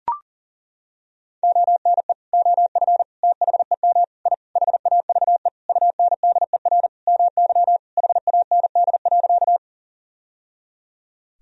Morse code, heard as a keyed tone, about 38 wpm, 700 Hz. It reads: ONE OF THEM I HAVE UNDER MY HAND.
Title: Sentences from Top 1000 Words - Review Part 1 - 40wpm